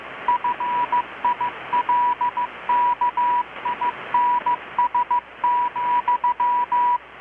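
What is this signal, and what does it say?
Morse code, heard as a keyed tone, about 15 wpm, 980 Hz, with radio noise. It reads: FILKINS,
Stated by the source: Morse from KPH's Night of Nights XX maritime radiotelegraphy event, received in Chiapas, Mexico about 3,800 km away on a Kenwood R-600 with a grounded 10 m long-wire antenna.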